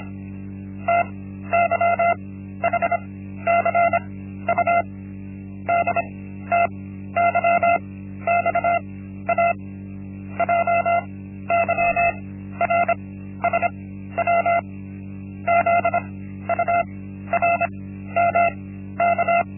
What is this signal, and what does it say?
Morse code, heard as a keyed tone, about 26 wpm, spaced as for 13 wpm, 690 Hz. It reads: TYHCU DTYXA JYRSW ZURMK
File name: Linksmakalnis-1